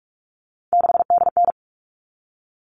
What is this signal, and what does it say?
Morse code, heard as a keyed tone, about 44 wpm, 700 Hz. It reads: 6DN
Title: All Cut Numbers - 3-Digit 45wpm